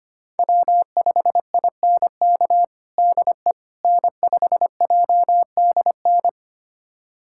Morse code, heard as a keyed tone, about 25 wpm, 700 Hz. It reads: W5INK DE N5JDN
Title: USA Callsign DE Callsign - 5 Characters - 25wpm